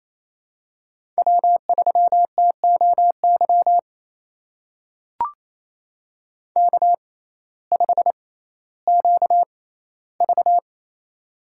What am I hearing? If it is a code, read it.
Morse code, heard as a keyed tone, about 28 wpm, spaced as for 15 wpm, 700 Hz. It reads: W3TOY K5QV
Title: USA Call Signs - 6x Character Spacing - 28wpm